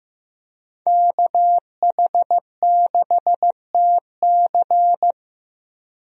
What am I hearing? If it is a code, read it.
Morse code, heard as a keyed tone, about 15 wpm, 700 Hz. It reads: KH6TC
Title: DXCC-100 Call Signs - 5 Characters - Repeated 2x - 15wpm